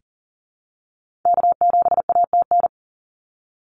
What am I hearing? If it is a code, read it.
Morse code, heard as a keyed tone, about 40 wpm, 700 Hz. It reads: K7ATN
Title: NA SOTA Activator Callsigns 40wpm